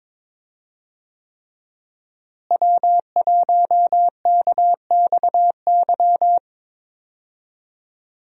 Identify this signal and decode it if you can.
Morse code, heard as a keyed tone, about 22 wpm, 700 Hz. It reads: W1KXY
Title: USA Call Signs - 5 Characters - Repeated 3x - 22wpm